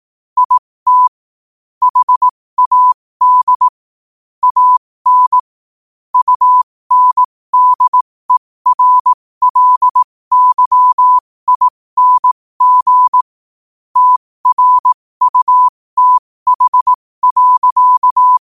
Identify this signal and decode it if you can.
Morse code, heard as a keyed tone, about 18 wpm, spaced as for 15 wpm, 1000 Hz. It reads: IT HAD AN UNDERLYING TRUTH.
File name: qotdfiles / 15wpm